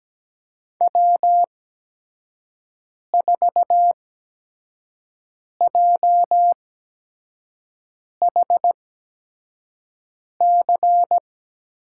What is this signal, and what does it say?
Morse code, heard as a keyed tone, about 17 wpm, spaced as for 5 wpm, 700 Hz. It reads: W4JHC
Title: USA Call Signs - 5 Characters - 8x Character Spacing - 17wpm